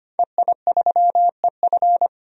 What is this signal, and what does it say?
Morse code, heard as a keyed tone, about 25 wpm, 700 Hz. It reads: EI3EF
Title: DXCC-100 Call Signs - 5 Characters - 25wpm